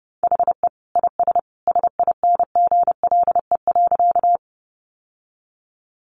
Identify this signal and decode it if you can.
Morse code, heard as a keyed tone, about 30 wpm, 700 Hz. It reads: HE IS SINGLE.
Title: Sentences from Top 700 Words 30wpm